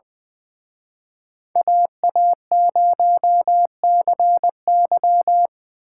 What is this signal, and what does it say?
Morse code, heard as a keyed tone, about 23 wpm, 700 Hz. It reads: AA0CY
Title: Speed Racing - USA Call Signs 20wpm